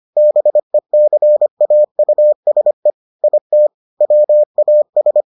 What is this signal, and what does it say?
Morse code, heard as a keyed tone, about 25 wpm, 600 Hz. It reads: BECAUSE IT WAS